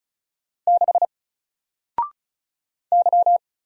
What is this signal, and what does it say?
Morse code, heard as a keyed tone, about 34 wpm, 700 Hz. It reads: 6 Y
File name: Single-Letter-Number-35wpm